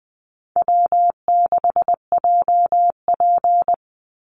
Morse code, heard as a keyed tone, about 20 wpm, 700 Hz. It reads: W6JP
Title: NA SOTA Chaser Callsigns 20wpm